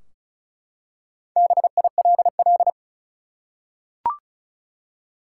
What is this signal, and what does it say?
Morse code, heard as a keyed tone, about 36 wpm, 700 Hz. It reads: BILL
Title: Four Letter Words - 35wpm